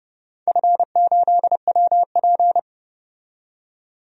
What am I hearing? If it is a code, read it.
Morse code, heard as a keyed tone, about 32 wpm, 700 Hz. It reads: F8WP
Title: DXCC-100 Call Signs - 4 Characters - 30wpm